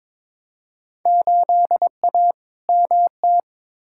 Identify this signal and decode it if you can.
Morse code, heard as a keyed tone, about 22 wpm, 700 Hz. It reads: 8A MT
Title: Field Day Exchanges Encore - 22wpm